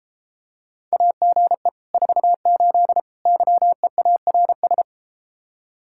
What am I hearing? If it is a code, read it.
Morse code, heard as a keyed tone, about 33 wpm, 700 Hz. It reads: AGE 48 YEARS